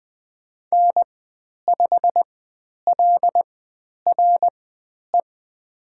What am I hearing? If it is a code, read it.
Morse code, heard as a keyed tone, about 20 wpm, spaced as for 10 wpm, 700 Hz. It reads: N5LRE
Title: 109 Lesson 18g - Letter m - Call Signs 20wpm